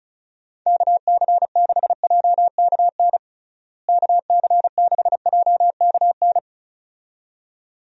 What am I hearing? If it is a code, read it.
Morse code, heard as a keyed tone, about 34 wpm, 700 Hz. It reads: KC6JKN KC6JKN
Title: USA Call Signs - 6 Characters - Repeated 2x - 35wpm